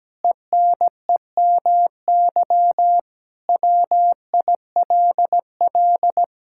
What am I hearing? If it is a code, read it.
Morse code, heard as a keyed tone, about 17 wpm, 700 Hz. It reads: ENEMY WILL
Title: Pangram Challenge v2 - 17wpm